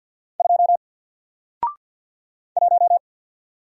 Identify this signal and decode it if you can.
Morse code, heard as a keyed tone, about 48 wpm, 700 Hz. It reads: 2 1
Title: Single Character (Letter:Number:Punctuation:Prosign) Alt 50wpm